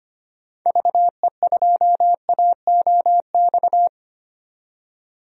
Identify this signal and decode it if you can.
Morse code, heard as a keyed tone, about 25 wpm, 700 Hz. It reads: VE2AOX
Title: DXCC-100 Call Signs - 6 Characters - 25wpm